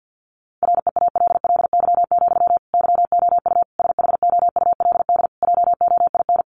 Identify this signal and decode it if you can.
Morse code, heard as a keyed tone, about 50 wpm, 700 Hz. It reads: REALLY, YOU SHOULD JOIN
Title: Sentences from Top 1400 Words 50wpm